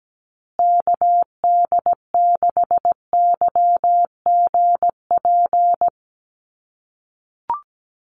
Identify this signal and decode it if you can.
Morse code, heard as a keyed tone, about 17 wpm, 700 Hz. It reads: KD6YGP
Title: USA Call Signs Repeated 3x 17wpm